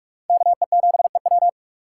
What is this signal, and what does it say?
Morse code, heard as a keyed tone, about 42 wpm, 700 Hz. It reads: KE7EW